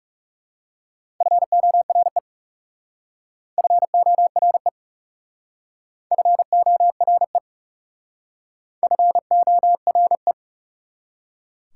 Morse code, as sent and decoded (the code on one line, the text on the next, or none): ..-. --- .-. / ..-. --- .-. / ..-. --- .-. / ..-. --- .-. .
FORE FORE FORE FORE